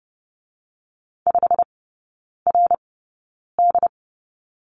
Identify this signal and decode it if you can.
Morse code, heard as a keyed tone, about 30 wpm, spaced as for 9 wpm, 700 Hz. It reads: 5RD
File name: Lesson 19e - Letter w - Three Characters 30wpm